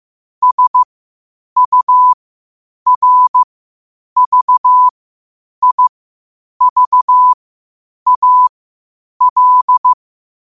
Morse code, heard as a keyed tone, about 15 wpm, spaced as for 8 wpm, 1000 Hz. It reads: SURVIVAL